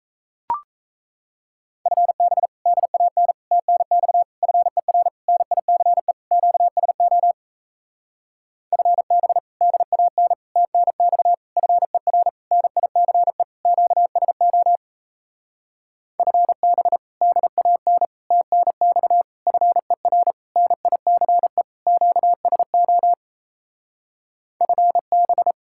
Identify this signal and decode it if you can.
Morse code, heard as a keyed tone, about 38 wpm, 700 Hz. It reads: FB DAN TNX FER NICE QSO FB DAN TNX FER NICE QSO FB DAN TNX FER NICE QSO FB